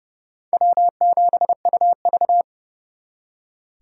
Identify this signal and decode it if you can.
Morse code, heard as a keyed tone, about 36 wpm, 700 Hz. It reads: W7UV
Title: Speed Racing - USA 1x2 Call Signs 30wpm